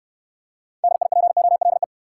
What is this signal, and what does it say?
Morse code, heard as a keyed tone, about 70 wpm, 700 Hz. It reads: DIPOLE